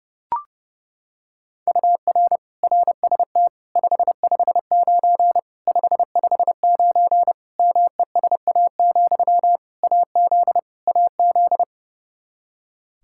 Morse code, heard as a keyed tone, about 30 wpm, 700 Hz. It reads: UR RST 559 559 MESA, AZ AZ
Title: Practice QSO - Exchange 1 - Beginning - 30wpm